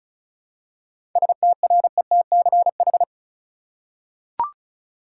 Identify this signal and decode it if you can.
Morse code, heard as a keyed tone, about 36 wpm, 700 Hz. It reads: STRETCH
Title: Seven Letter Words - 35wpm